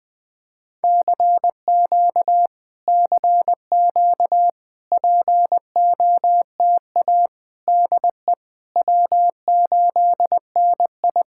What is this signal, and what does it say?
Morse code, heard as a keyed tone, about 20 wpm, 700 Hz. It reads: CQ CQ POTA DE W8NI
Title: POTA QSO Practice - 20wpm